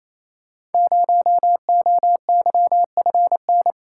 Morse code, heard as a keyed tone, about 28 wpm, 700 Hz. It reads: 0OYFN
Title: Code Groups - 5 Letter:Numbers - 28wpm